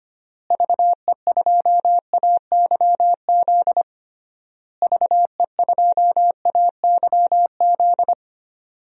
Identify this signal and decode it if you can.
Morse code, heard as a keyed tone, about 25 wpm, 700 Hz. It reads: VE2AYZ VE2AYZ
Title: DXCC-100 Call Signs - 6 Characters - Repeated 3x - 25wpm